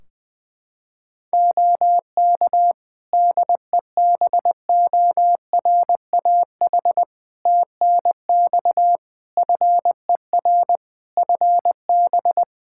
Morse code, heard as a keyed tone, about 20 wpm, 700 Hz. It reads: OK DEBORAH TNX FER FB
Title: Practice QSO - Exchange 4 - Ending - 20wpm